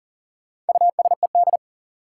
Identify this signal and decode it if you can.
Morse code, heard as a keyed tone, about 40 wpm, 700 Hz. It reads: USED
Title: Sentences from Top 600 Words - New Word Drill - 40wpm